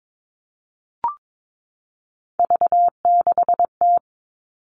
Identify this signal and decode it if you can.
Morse code, heard as a keyed tone, about 22 wpm, 700 Hz. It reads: V6T
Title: All Cut Numbers - 3-Digit 22wpm